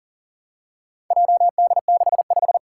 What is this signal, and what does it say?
Morse code, heard as a keyed tone, about 38 wpm, 700 Hz. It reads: JD65
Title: Four Challenging Letter-Number Combinations 40wpm